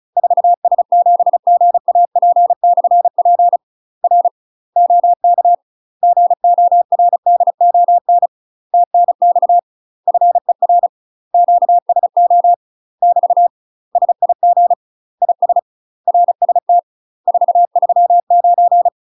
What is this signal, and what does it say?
Morse code, heard as a keyed tone, about 37 wpm, 700 Hz. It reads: 4S7GAP/P R OK GORDON TNX FER QSO <BT> SIG IS RST 439
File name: UK MASTER QSOs TRAINING / 35WPM / UK MASTER QSOs 0030